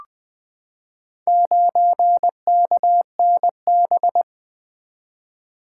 Morse code, heard as a keyed tone, about 20 wpm, 700 Hz. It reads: 9KNB
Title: Four Letter-Number Combinations - 20wpm